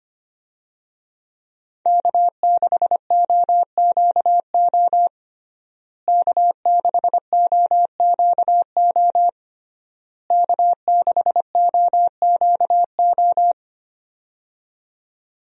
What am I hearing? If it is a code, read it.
Morse code, heard as a keyed tone, about 25 wpm, 700 Hz. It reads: K6OQO K6OQO K6OQO
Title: USA Call Signs - 5 Characters - Repeated 3x - 25wpm